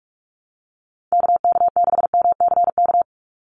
Morse code, heard as a keyed tone, about 44 wpm, 700 Hz. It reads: KK6MCK
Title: USA Call Signs 45wpm